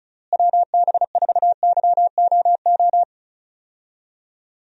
Morse code, heard as a keyed tone, about 34 wpm, 700 Hz. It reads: WB4YOO
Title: USA Call Signs - 6 Characters - Repeated 3x - 35wpm